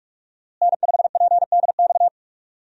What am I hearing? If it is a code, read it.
Morse code, heard as a keyed tone, about 41 wpm, 700 Hz. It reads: N5PDX
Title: USA Call Signs - 5 Characters - Repeated 3x - 45wpm